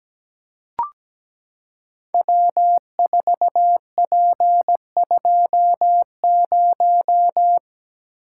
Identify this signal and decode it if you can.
Morse code, heard as a keyed tone, about 17 wpm, 700 Hz. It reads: W4P20
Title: Code Groups - 5 Letter:Numbers - 17wpm